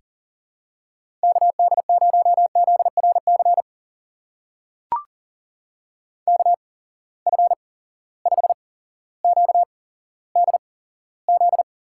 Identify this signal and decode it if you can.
Morse code, heard as a keyed tone, about 40 wpm, spaced as for 17 wpm, 700 Hz. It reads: KD0ZRC KF5QDZ
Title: USA Call Signs - 6 Characters - 8x Character Spacing - 40wpm